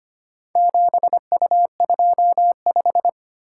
Morse code, heard as a keyed tone, about 25 wpm, 700 Hz. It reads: 7U25
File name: practice / 25wpm / Four Letter-Number Combinations - 25wpm